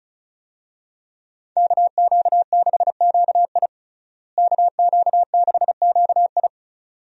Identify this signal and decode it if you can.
Morse code, heard as a keyed tone, about 36 wpm, 700 Hz. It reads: KQ6QI KQ6QI
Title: USA Call Signs - 5 Characters - Repeated 2x - 35wpm